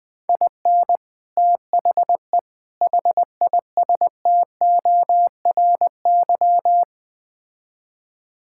Morse code, heard as a keyed tone, about 20 wpm, 700 Hz. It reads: IN THE HISTORY
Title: Build A Phrase - 4-Word Phrases - 20wpm